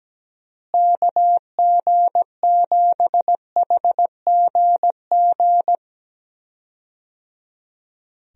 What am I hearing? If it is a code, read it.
Morse code, heard as a keyed tone, about 17 wpm, 700 Hz. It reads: KG7HGG
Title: USA Call Signs - 6 Characters - Repeated 3x - 17wpm